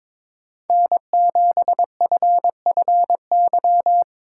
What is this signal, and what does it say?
Morse code, heard as a keyed tone, about 21 wpm, 700 Hz. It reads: N7FFY
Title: USA Call Signs - 5 Characters - 6x Character Spacing - 22wpm